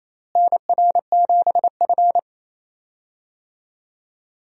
Morse code, heard as a keyed tone, about 28 wpm, 700 Hz. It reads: NR7F
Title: USA Call Signs - 4 Characters - Repeated 2x - 28wpm